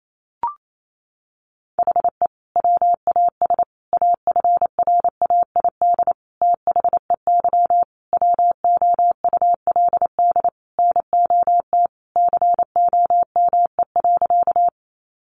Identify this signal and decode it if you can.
Morse code, heard as a keyed tone, about 28 wpm, 700 Hz. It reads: HE WAS AFRAID THEY WOULD NOT COME.
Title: Sentences from Top 1500 Words 28wpm